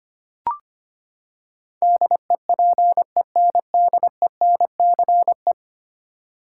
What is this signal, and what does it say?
Morse code, heard as a keyed tone, about 25 wpm, 700 Hz. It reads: DEPENDENCE
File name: Ten Letter Words - 25wpm